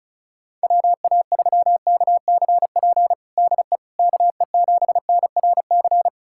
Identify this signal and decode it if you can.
Morse code, heard as a keyed tone, about 36 wpm, 700 Hz. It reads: WA3KCP DE KE7NRC
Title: USA Callsign DE Callsign - 6 Characters - 35wpm